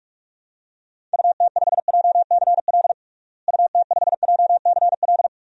Morse code, heard as a keyed tone, about 41 wpm, 700 Hz. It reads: UT5JCL UT5JCL
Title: DXCC-100 Call Signs - 6 Characters - Repeated 2x - 45wpm